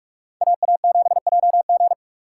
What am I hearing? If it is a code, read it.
Morse code, heard as a keyed tone, about 44 wpm, 700 Hz. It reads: AA7JG